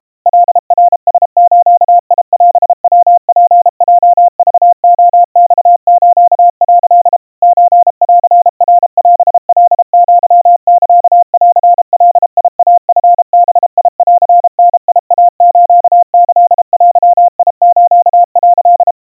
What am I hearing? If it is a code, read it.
Morse code (Japanese Wabun code), heard as a keyed tone, about 32 wpm, 700 Hz. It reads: カナラズオヤツヲクレマス」 ソンナオカアサンガイチバンダイスキデス」
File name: kawa / Morse sample / kamyu 120 cpm 700 Hz Crispy